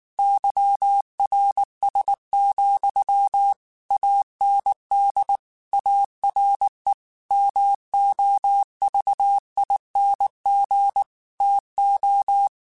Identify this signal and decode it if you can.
Morse code, heard as a keyed tone, about 19 wpm, 800 Hz. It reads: YRS, AND ARE MOVING TO
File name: QSOs / QSOs-mp3s / WD0BGP 1@18